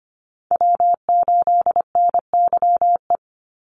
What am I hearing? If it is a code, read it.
Morse code, heard as a keyed tone, about 25 wpm, 700 Hz. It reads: W8NYE
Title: USA Call Signs 25wpm